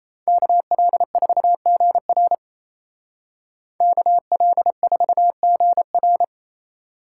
Morse code, heard as a keyed tone, about 34 wpm, 700 Hz. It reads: KL4GR KL4GR
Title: Speed Racing - USA 2x2 Call Signs 28wpm